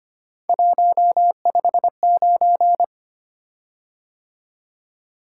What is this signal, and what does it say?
Morse code, heard as a keyed tone, about 25 wpm, 700 Hz. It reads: 159
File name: Three Letter-Number Combinations - 25wpm